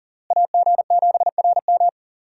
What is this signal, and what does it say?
Morse code, heard as a keyed tone, about 37 wpm, 700 Hz. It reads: AG7RM
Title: USA Call Signs - 5 Characters - Repeated 3x - 40wpm